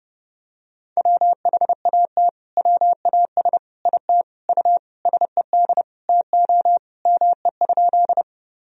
Morse code, heard as a keyed tone, about 32 wpm, 700 Hz. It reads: WHAT WAS IT U SED TO ME?
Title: Sentences from Top 200 Words and Abbreviations 30wpm